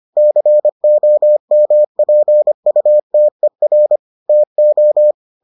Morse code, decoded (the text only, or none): COMPUTER TO